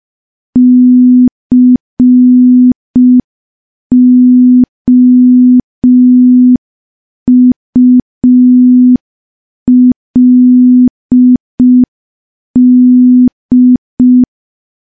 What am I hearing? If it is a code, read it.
Morse code, heard as a keyed tone, about 5 wpm, 250 Hz. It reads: COULD
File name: morse / morse-training-05WPM